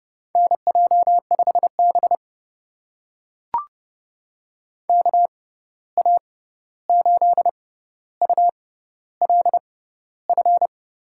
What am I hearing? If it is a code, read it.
Morse code, heard as a keyed tone, about 30 wpm, spaced as for 14 wpm, 700 Hz. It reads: NJ5B KA8ULF
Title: USA Call Signs - 6x Character Spacing - 30wpm